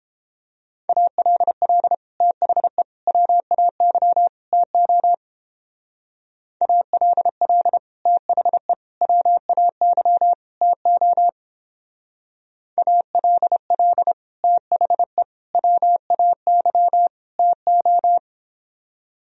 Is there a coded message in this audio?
Yes: Morse code from 0.9 to 18.2 s, heard as a keyed tone, about 29 wpm, 700 Hz. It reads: ALL THE WAY TO ALL THE WAY TO ALL THE WAY TO